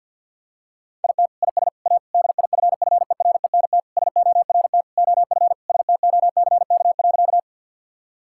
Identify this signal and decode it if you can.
Morse code, heard as a keyed tone, about 50 wpm, 700 Hz. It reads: IT IS A DIFFERENT SORT OF STOCK.
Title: Sentences from Top 1000 Words - Review Part 5 - 50wpm